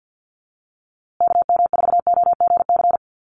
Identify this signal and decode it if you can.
Morse code, heard as a keyed tone, about 48 wpm, 700 Hz. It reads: KM4OGC
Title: USA Call Signs Repeated 3x 50wpm